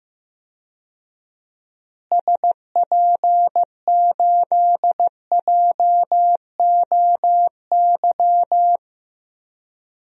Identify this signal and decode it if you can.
Morse code, heard as a keyed tone, about 15 wpm, 700 Hz. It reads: SP8JOY
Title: DXCC-100 Call Signs - 6 Characters - Repeated 2x - 15wpm